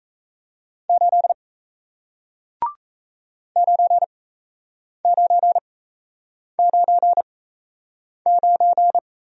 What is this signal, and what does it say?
Morse code, heard as a keyed tone, about 37 wpm, spaced as for 8 wpm, 700 Hz. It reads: 8 9999